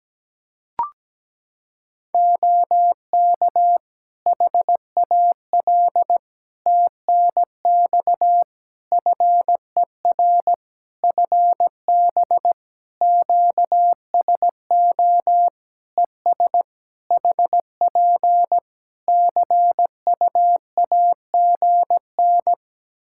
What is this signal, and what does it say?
Morse code, heard as a keyed tone, about 17 wpm, 700 Hz. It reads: OK HAL TNX FER FB QSO ES HP CUAGN
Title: Practice QSO - Exchange 4 - Ending - 17wpm